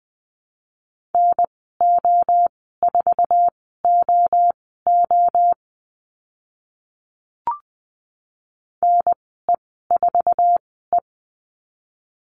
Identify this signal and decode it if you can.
Morse code, heard as a keyed tone, about 20 wpm, spaced as for 14 wpm, 700 Hz. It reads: NO4OO NE4E
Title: 055 Lesson 10g - Letter r - Call Signs 20wpm